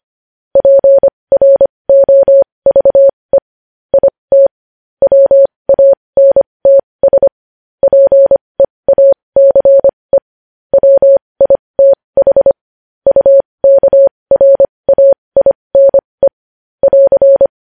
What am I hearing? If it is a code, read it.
Morse code, heard as a keyed tone, about 25 wpm, spaced as for 20 wpm, 550 Hz. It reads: PROVE IT WANTS PEACE WITH UKRAINE <AR>
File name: podcast / News headlines-1x-25x20